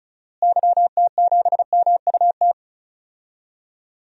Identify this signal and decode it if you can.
Morse code, heard as a keyed tone, about 36 wpm, 700 Hz. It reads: YT7MUT
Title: DXCC-100 Call Signs - 6 Characters - Repeated 3x - 35wpm